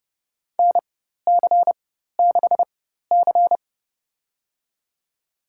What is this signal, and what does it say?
Morse code, heard as a keyed tone, about 30 wpm, spaced as for 14 wpm, 700 Hz. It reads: NC6C
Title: USA Call Signs - 4 Characters - 4x Character Spacing - 30wpm